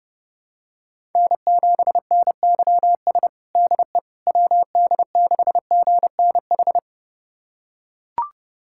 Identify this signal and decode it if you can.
Morse code, heard as a keyed tone, about 30 wpm, 700 Hz. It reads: N7NYS DE WD6GNH
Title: USA Callsign DE Callsign 30wpm